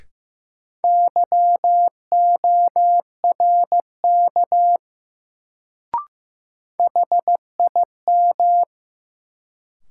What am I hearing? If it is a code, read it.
Morse code, heard as a keyed tone, about 15 wpm, 700 Hz. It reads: YORK HIM
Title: Sets of 2 Words - New Word Drill - 15wpm